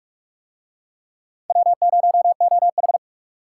Morse code, heard as a keyed tone, about 40 wpm, 700 Hz. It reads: W0OH